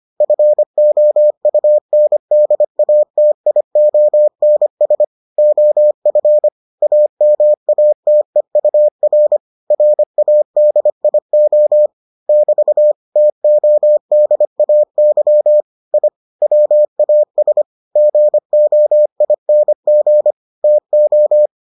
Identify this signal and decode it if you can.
Morse code, heard as a keyed tone, about 25 wpm, 600 Hz. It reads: FOUNDATIONS OF AMATEUR RADIO <BT> TODAY I WAS GOING TO